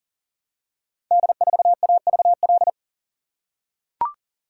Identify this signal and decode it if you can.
Morse code, heard as a keyed tone, about 41 wpm, 700 Hz. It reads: D4AVL
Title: Code Groups - 5 Letter:Numbers - 40wpm